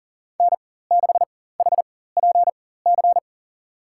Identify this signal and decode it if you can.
Morse code, heard as a keyed tone, about 40 wpm, spaced as for 19 wpm, 700 Hz. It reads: N6HPC
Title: USA Call Signs - 5 Characters - 4x Character Spacing - 40wpm